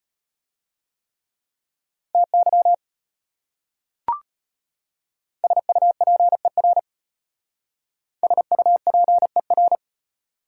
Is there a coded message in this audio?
Yes: Morse code from 2.1 to 9.8 s, heard as a keyed tone, about 37 wpm, 700 Hz. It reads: TY SUPER SUPER